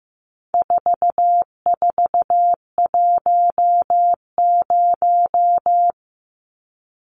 Morse code, heard as a keyed tone, about 15 wpm, 700 Hz. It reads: 4410
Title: Numbers - 4-Digit 15wpm